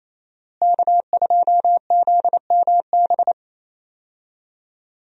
Morse code, heard as a keyed tone, about 28 wpm, 700 Hz. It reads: K2ZMB
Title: USA Call Signs - 5 Characters - Repeated 3x - 28wpm